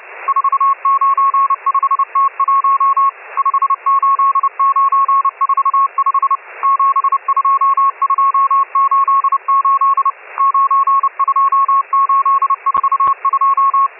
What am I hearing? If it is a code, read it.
Morse code, heard as a keyed tone, about 30 wpm, 1100 Hz, with radio noise. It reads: 495T1 58945 72288 92842
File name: M12- -1010utc